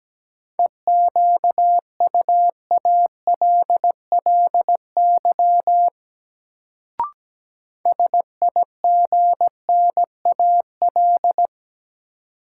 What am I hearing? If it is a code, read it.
Morse code, heard as a keyed tone, about 17 wpm, 700 Hz. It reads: EQUALLY SIGNAL
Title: Sentences from Top 1400 Words - New Word Drill Alt - 17wpm